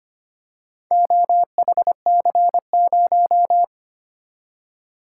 Morse code, heard as a keyed tone, about 25 wpm, 700 Hz. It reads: OHC0
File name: practice / 25wpm / Four Letter-Number Combinations - 25wpm